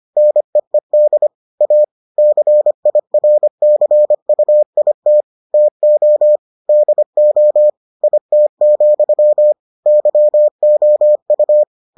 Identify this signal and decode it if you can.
Morse code, heard as a keyed tone, about 25 wpm, 600 Hz. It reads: NEED A CIRCUIT TO DO IT, YOU